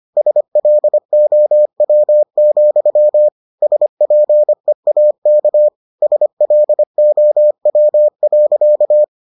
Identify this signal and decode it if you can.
Morse code, heard as a keyed tone, about 25 wpm, 600 Hz. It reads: SLOW, SPEAK SLOW.